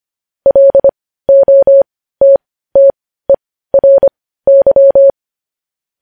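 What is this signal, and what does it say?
Morse code, heard as a keyed tone, about 25 wpm, spaced as for 15 wpm, 550 Hz. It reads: LOTTERY
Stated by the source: Morse code news headlines sent at 15 wpm.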